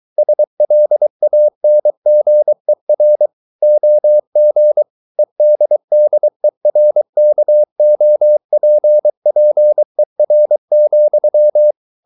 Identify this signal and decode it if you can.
Morse code, heard as a keyed tone, about 23 wpm, 600 Hz. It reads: SLANGER OG EDDERKOPPER,
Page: Ferie 23 wpm